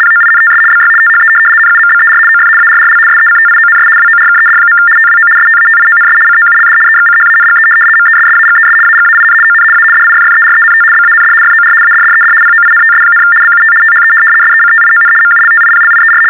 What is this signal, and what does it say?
RTTY, about 75 baud. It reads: POINTS BY THE NUMBER OF ARRL SECTIONS WORKED. TWO STATIONS COULD EXCHANGE MESSAGES AGAIN ON A DIFFERENT BAND FOR ADDED POINTS, BUT THE SECTION MULTIPLIER
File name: teletype-RTTY-75